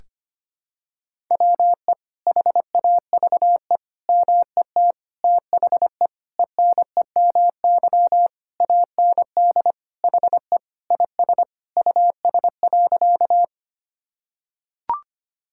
Morse code, heard as a keyed tone, about 25 wpm, 700 Hz. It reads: WE HAVE MET THE ENEMY AND HE IS US.
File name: practice / 25wpm / Sentences with 9-Word Phrases - 25wpm